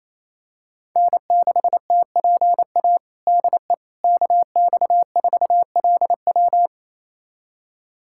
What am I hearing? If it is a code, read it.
Morse code, heard as a keyed tone, about 28 wpm, 700 Hz. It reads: N6TPA DE KX4LW
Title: USA Callsign DE Callsign - 5 Characters - 28wpm